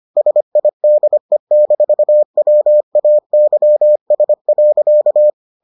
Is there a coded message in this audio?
Yes: Morse code from 0.2 to 5.3 s, heard as a keyed tone, about 25 wpm, 600 Hz. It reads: SIDE-WAYS.